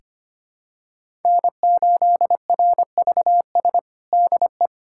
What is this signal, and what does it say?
Morse code, heard as a keyed tone, about 25 wpm, 700 Hz. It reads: N8RVS DE
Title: USA Callsign DE Callsign - 5 Characters - 25wpm